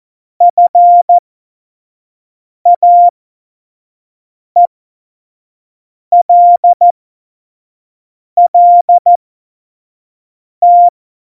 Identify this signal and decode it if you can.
Morse code, heard as a keyed tone, about 14 wpm, spaced as for 5 wpm, 700 Hz. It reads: FAELLT
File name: Morsekurs / Witze / Witz 24. November 2020, 25 F70